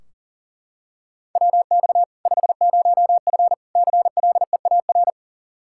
Morse code, heard as a keyed tone, about 40 wpm, 700 Hz. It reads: WX 50F CLEAR
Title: Practice QSO - Exchange 2 - Info - 40wpm